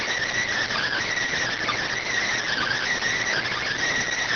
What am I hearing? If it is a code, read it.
Emergency Alert System (SAME) tones, heard as harsh buzzing bursts. It is heard from 0.7 to 3.5 s.